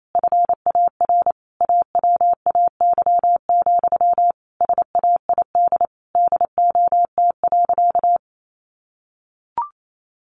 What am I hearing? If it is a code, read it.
Morse code, heard as a keyed tone, about 28 wpm, 700 Hz. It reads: FAR AWAY, SAID DOT.
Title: Sentences from Top 1500 Words 28wpm